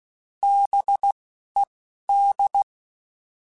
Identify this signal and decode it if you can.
Morse code, heard as a keyed tone, about 16 wpm, spaced as for 11 wpm, 800 Hz. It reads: BED
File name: Alphabetical / Alphabetical mp3s / Bs @10